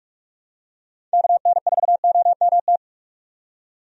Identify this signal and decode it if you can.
Morse code, heard as a keyed tone, about 46 wpm, 700 Hz. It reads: KN4OMT
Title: USA Call Signs - 6 Characters - 45wpm